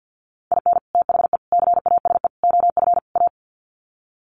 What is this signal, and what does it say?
Morse code, heard as a keyed tone, about 50 wpm, 700 Hz. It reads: IN THE CASE OF A
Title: Sets of 5 Words Encore 50wpm